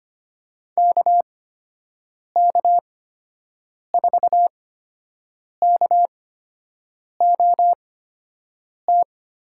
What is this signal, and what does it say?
Morse code, heard as a keyed tone, about 25 wpm, spaced as for 7 wpm, 700 Hz. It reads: KK4KOT
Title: USA Call Signs - 6 Characters - 8x Character Spacing - 25wpm